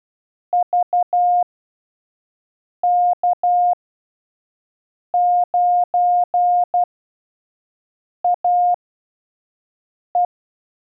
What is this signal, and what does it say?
Morse code, heard as a keyed tone, about 12 wpm, spaced as for 5 wpm, 710 Hz. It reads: VK9AE
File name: 5WPM test 1